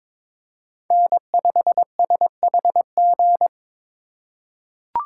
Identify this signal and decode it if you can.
Morse code, heard as a keyed tone, about 22 wpm, 700 Hz. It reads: N5SHG
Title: USA Call Signs - 5 Characters - 4x Character Spacing - 22wpm